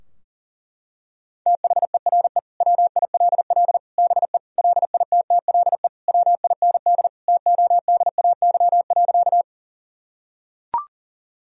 Morse code, heard as a keyed tone, about 37 wpm, 700 Hz. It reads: THERE WILL BE LITTLE WIND TODAY.
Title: Sentences from Top 1000 Words - Review Part 2 - 40wpm